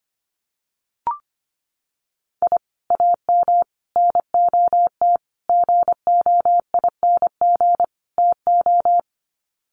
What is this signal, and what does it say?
Morse code, heard as a keyed tone, about 25 wpm, 690 Hz. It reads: I AM NOT GOING TO
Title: Sets of 5 Words Encore 25wpm